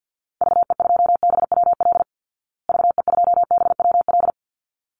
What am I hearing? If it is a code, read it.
Morse code, heard as a keyed tone, about 48 wpm, 700 Hz. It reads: VE2BWL VE2BWL
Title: Call Signs Repeated 3x - 50wpm